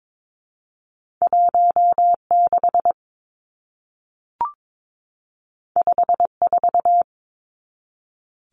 Morse code, heard as a keyed tone, about 22 wpm, 700 Hz. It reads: 16 54
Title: Numbers - 2-Digit 22wpm